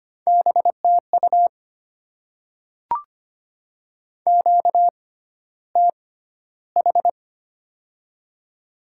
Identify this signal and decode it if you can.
Morse code, heard as a keyed tone, about 25 wpm, spaced as for 13 wpm, 690 Hz. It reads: BTU QTH
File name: Top 20 QSO Elements - 6x Character Spacing - 25wpm